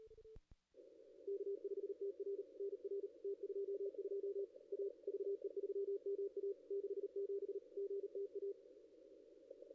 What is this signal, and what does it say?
Morse code, heard as a keyed tone, about 38 wpm, 440 Hz. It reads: K5TR NR T22 A VE3MA 67 GTA
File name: contests / ss-cw-k5tr-op-n5rz -2200